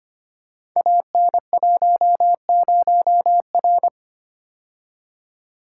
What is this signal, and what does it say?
Morse code, heard as a keyed tone, about 25 wpm, 700 Hz. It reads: AN10R
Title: Code Groups - 5 Letter:Numbers - 25wpm